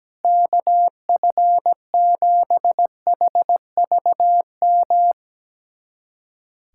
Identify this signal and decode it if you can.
Morse code, heard as a keyed tone, about 17 wpm, 700 Hz. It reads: KF7HVM
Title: USA Callsign DE Callsign 17wpm